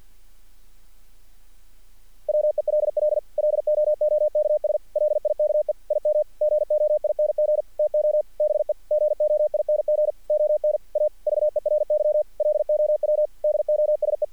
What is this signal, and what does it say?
Morse code, heard as a keyed tone, about 50 wpm, 600 Hz. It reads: WELL LOOKS LIKE IM GOING TO BE GOING ON A VERY LOW DOSE
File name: archived / CRSnet